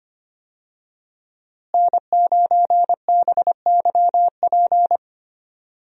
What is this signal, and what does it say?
Morse code, heard as a keyed tone, about 25 wpm, 700 Hz. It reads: N9BYP
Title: USA Call Signs - 5 Characters - Repeated 2x - 25wpm